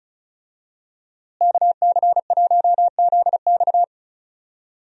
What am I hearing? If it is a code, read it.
Morse code, heard as a keyed tone, about 34 wpm, 700 Hz. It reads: KC1ZX
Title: USA Call Signs - 5 Characters - Repeated 2x - 35wpm